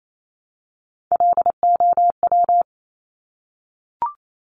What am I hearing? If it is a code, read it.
Morse code, heard as a keyed tone, about 28 wpm, 700 Hz. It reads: LOW
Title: Top Three Letter Combinations in Words 28wpm